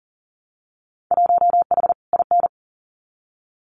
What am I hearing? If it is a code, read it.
Morse code, heard as a keyed tone, about 42 wpm, 700 Hz. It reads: 1H IN
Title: Winter Field Day Exchanges 40wpm